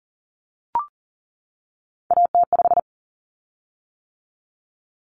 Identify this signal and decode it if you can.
Morse code, heard as a keyed tone, about 40 wpm, 700 Hz. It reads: AT5
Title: Common Cut Numbers - 3-Digit 40wpm